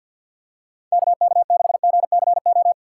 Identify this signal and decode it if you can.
Morse code, heard as a keyed tone, about 53 wpm, 700 Hz. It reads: KK6GCO
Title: USA Call Signs - 6 Characters - 3x Character Spacing - 50wpm